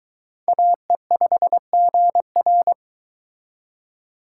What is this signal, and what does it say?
Morse code, heard as a keyed tone, about 22 wpm, 700 Hz. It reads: AE5GR